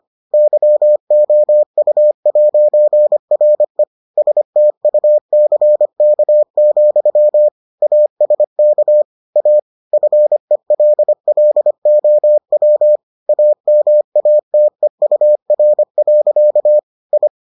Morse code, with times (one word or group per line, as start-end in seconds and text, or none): 0.3-3.8 YOU'RE
4.2-7.5 STUCK,
7.8-9.0 ASK
9.4-9.6 A
9.9-13.0 FELLOW
13.3-16.8 AMATEUR.
17.1-17.3 I